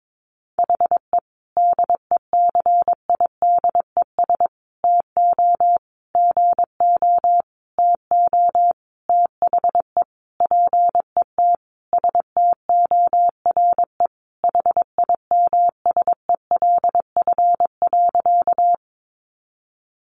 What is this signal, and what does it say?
Morse code, heard as a keyed tone, about 22 wpm, 700 Hz. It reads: HE DECIDES TO GO TO THE PET STORE HIMSELF.
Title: Easy Phrases 22wpm